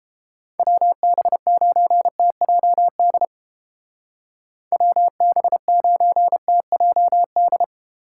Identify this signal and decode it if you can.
Morse code, heard as a keyed tone, about 29 wpm, 700 Hz. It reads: WB9TJD WB9TJD